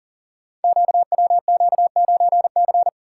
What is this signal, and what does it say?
Morse code, heard as a keyed tone, about 40 wpm, 700 Hz. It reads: QWQ9C
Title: Code Groups - 5 Letter:Numbers - 40wpm